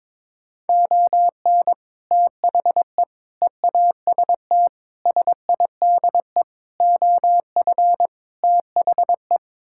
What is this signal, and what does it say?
Morse code, heard as a keyed tone, about 22 wpm, 700 Hz. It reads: ON THE EAST SIDE OF THE
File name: practice / 22wpm / Sets of 6 Words - 22wpm